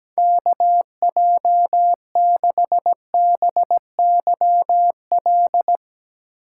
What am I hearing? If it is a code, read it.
Morse code, heard as a keyed tone, about 17 wpm, 700 Hz. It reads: KJ6BYL
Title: Speed Racing - USA Call Signs 17wpm